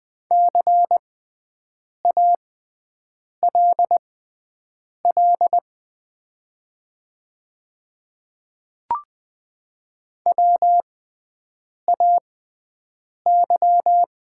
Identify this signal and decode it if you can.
Morse code, heard as a keyed tone, about 20 wpm, spaced as for 7 wpm, 700 Hz. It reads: CALL WAY
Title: 6x Character Spacing - Top 100 Words - 20wpm